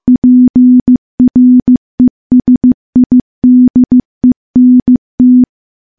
Morse code, heard as a keyed tone, about 15 wpm, 250 Hz. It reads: PRESIDENT
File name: morse-training-15WPM